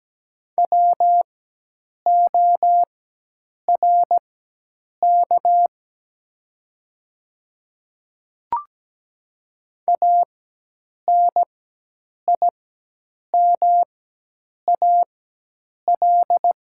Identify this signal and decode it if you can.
Morse code, heard as a keyed tone, about 17 wpm, spaced as for 8 wpm, 700 Hz. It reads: WORK ANIMAL
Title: 4x Character Spacing - Top 300 Words - 17wpm